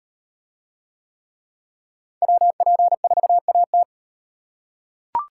Morse code, heard as a keyed tone, about 35 wpm, 700 Hz. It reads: WP4AT